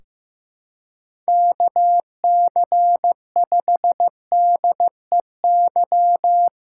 Morse code, heard as a keyed tone, about 15 wpm, 700 Hz. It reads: KC5DEY
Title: Practice QSO - Exchange 4 - Ending - 15wpm